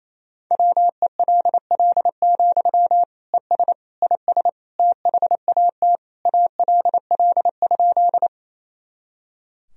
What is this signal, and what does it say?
Morse code, heard as a keyed tone, about 28 wpm, 700 Hz. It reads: WELL, ES IS THAT ALL?
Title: Sentences from Top 200 Words and Abbreviations 28wpm